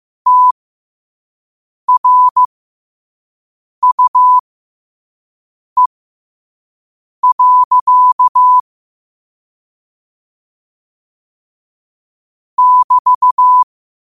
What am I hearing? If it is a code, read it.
Morse code, heard as a keyed tone, about 15 wpm, spaced as for 5 wpm, 1000 Hz. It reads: TRUE. <BT>
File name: qotdfiles / 5wpm